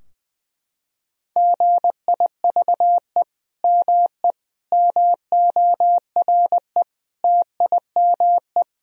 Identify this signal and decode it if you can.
Morse code, heard as a keyed tone, about 20 wpm, 700 Hz. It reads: GIVE ME MORE TIME
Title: Sentences from Top 1000 Words - Review Part 3 - 20wpm